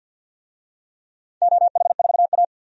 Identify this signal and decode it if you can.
Morse code, heard as a keyed tone, about 48 wpm, 700 Hz. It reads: OH4A
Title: DXCC-100 Call Signs - 4 Characters - 50wpm